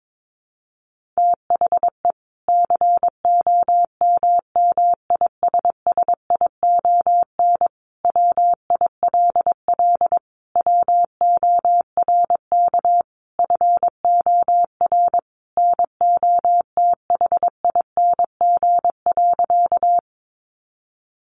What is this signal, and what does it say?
Morse code, heard as a keyed tone, about 22 wpm, 700 Hz. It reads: THE COMMISSION WILL WORK FOR NOTHING.
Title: Sentences from Top 1400 Words 22wpm